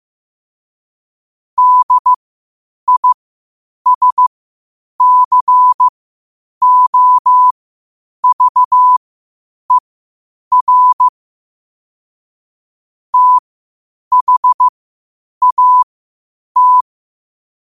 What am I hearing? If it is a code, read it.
Morse code, heard as a keyed tone, about 15 wpm, spaced as for 8 wpm, 1000 Hz. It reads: DISCOVER THAT
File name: qotdfiles / 8wpm